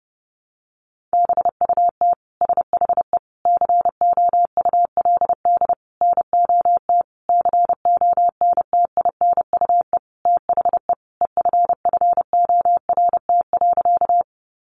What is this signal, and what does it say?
Morse code, heard as a keyed tone, about 30 wpm, 700 Hz. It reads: BUT SHE COULD NOT CONTINUE THE EFFORT.
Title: Sentences from Top 1200 Words 30wpm